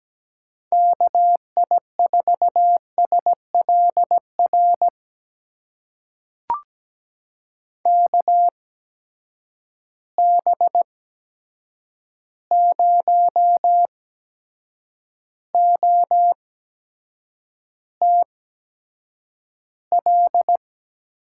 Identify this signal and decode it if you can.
Morse code, heard as a keyed tone, about 17 wpm, spaced as for 7 wpm, 700 Hz. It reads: KI4SLR KB0OTL
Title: USA Call Signs - 6 Characters - 8x Character Spacing - 17wpm